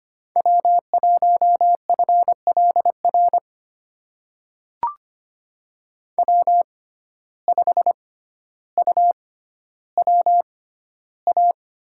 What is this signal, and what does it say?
Morse code, heard as a keyed tone, about 25 wpm, spaced as for 13 wpm, 700 Hz. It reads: W1FLR W5UWA
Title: USA Call Signs - 5 Characters - 6x Character Spacing - 25wpm